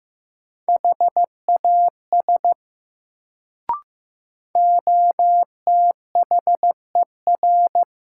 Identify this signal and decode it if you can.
Morse code, heard as a keyed tone, about 15 wpm, 700 Hz. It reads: HAS OTHER
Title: Sets of 2 Words - New Word Drill - 15wpm